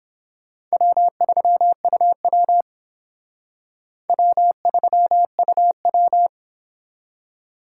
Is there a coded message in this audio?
Yes: Morse code from 0.7 to 6.3 s, heard as a keyed tone, about 30 wpm, 700 Hz. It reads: W3UW W3UW